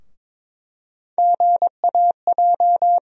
Morse code, heard as a keyed tone, about 22 wpm, 700 Hz. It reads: GAJ
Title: USA Call Sign Postfixes - 3 Letters 22wpm